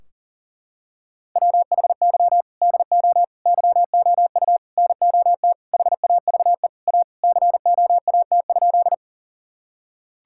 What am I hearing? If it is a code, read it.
Morse code, heard as a keyed tone, about 40 wpm, 700 Hz. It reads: WHY DO YOU NOT HAVE A COAT?
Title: Sentences from Top 1600 Words 40wpm